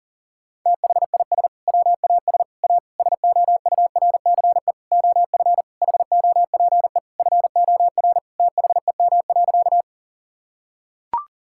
Morse code, heard as a keyed tone, about 40 wpm, 700 Hz. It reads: THIS WAS A SOURCE OF HOPE FOR THEM.
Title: Sentences from Top 1000 Words - Review Part 5 - 40wpm